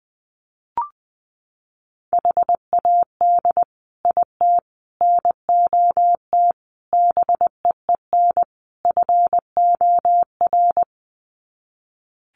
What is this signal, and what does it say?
Morse code, heard as a keyed tone, about 20 wpm, 700 Hz. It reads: HAD IT NOT BEEN FOR
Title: Sets of 5 Words Encore2 20wpm